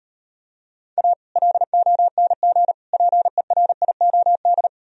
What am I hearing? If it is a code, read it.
Morse code, heard as a keyed tone, about 38 wpm, 700 Hz. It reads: A LONG PERIOD